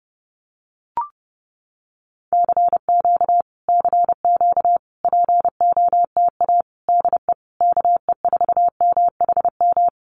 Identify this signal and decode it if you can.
Morse code, heard as a keyed tone, about 32 wpm, 700 Hz. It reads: CQ CQ POTA DE KE4MHM
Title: Common QSO Phrases Encore 30wpm